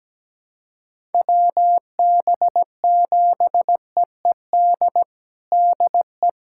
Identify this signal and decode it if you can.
Morse code, heard as a keyed tone, about 17 wpm, 700 Hz. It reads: WB7EED DE
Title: USA Callsign DE Callsign 17wpm